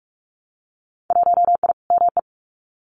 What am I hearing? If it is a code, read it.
Morse code, heard as a keyed tone, about 46 wpm, 700 Hz. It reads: 1I ME
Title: Winter Field Day Exchanges 45wpm